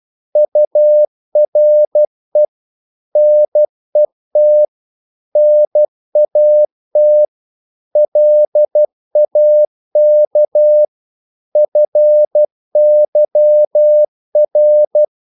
Morse code, heard as a keyed tone, about 12 wpm, 600 Hz. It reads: URE NET NAT LAK FYR
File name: Korte ord DK 12wpm